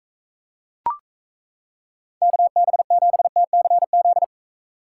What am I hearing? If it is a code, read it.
Morse code, heard as a keyed tone, about 34 wpm, 700 Hz. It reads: KB7TCZ
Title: Speed Racing - USA 2x3 Call Signs 28wpm